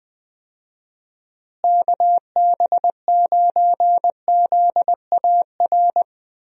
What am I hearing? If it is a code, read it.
Morse code, heard as a keyed tone, about 20 wpm, 700 Hz. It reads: KB9ZAR
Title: USA Call Signs - 6 Characters - Repeated 3x - 20wpm